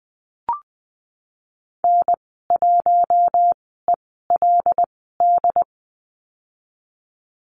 Morse code, heard as a keyed tone, about 20 wpm, spaced as for 15 wpm, 700 Hz. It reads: N1ELD
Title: 088 Lesson 15g - Number 5 - Call Signs 20wpm